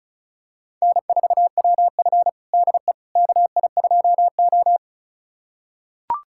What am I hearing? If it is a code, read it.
Morse code, heard as a keyed tone, about 36 wpm, 690 Hz. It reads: N4WF DE KI2O
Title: USA Callsign DE Callsign - 4 Characters - 35wpm